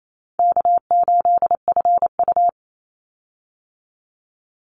Morse code, heard as a keyed tone, about 28 wpm, 700 Hz. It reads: K8FU
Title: Easy Call Signs Repeated 3x - 28wpm